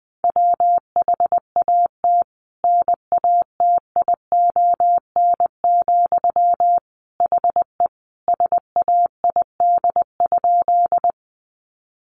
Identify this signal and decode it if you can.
Morse code, heard as a keyed tone, about 20 wpm, 700 Hz. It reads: WHAT NATION, HE SAID?
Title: Sentences from Top 900 Words 20wpm